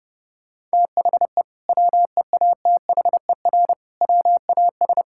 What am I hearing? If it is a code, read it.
Morse code, heard as a keyed tone, about 28 wpm, 700 Hz. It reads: THE WEATHER WAS
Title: Sentences from Top 1000 Words - Review Part 5 - 30wpm